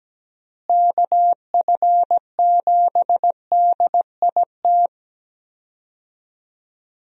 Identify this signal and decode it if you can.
Morse code, heard as a keyed tone, about 17 wpm, 700 Hz. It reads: KF7DIT
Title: USA Call Signs - 6 Characters - Repeated 3x - 17wpm